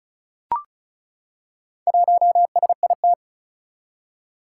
Morse code, heard as a keyed tone, about 36 wpm, 700 Hz. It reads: 1SIT
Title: Four Challenging Letter-Number Combinations 35wpm